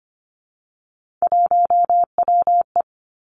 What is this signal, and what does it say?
Morse code, heard as a keyed tone, about 25 wpm, 700 Hz. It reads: 1WE
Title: Three Challenging Letter-Number Combinations 25wpm